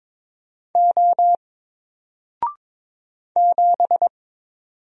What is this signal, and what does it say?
Morse code, heard as a keyed tone, about 22 wpm, 700 Hz. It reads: O 7
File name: Single-Letter-Number-22wpm